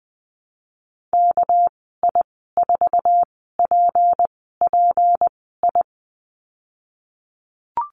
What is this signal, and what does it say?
Morse code, heard as a keyed tone, about 20 wpm, spaced as for 15 wpm, 700 Hz. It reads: KI4PPI
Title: 238 Lesson 38g - Letter q - Call Signs 20wpm